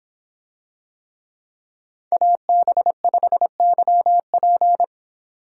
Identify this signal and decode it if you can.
Morse code, heard as a keyed tone, about 25 wpm, 700 Hz. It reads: AB5YP